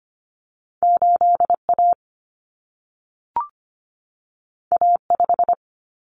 Morse code, heard as a keyed tone, about 25 wpm, 700 Hz. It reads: 8A A5
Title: Common Cut Numbers - 2-Digit 25wpm